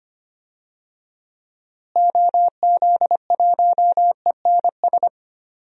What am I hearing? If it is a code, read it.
Morse code, heard as a keyed tone, about 25 wpm, 700 Hz. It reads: OZ1ENS
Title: DXCC-100 Call Signs - 6 Characters - Repeated 2x - 25wpm